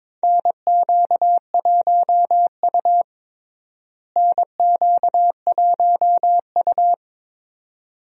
Morse code, heard as a keyed tone, about 22 wpm, 700 Hz. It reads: NQ1U NQ1U
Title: USA Call Signs - 4 Characters - Repeated 2x - 22wpm